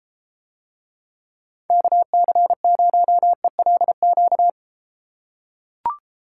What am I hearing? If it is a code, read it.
Morse code, heard as a keyed tone, about 34 wpm, 700 Hz. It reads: KC0ELQ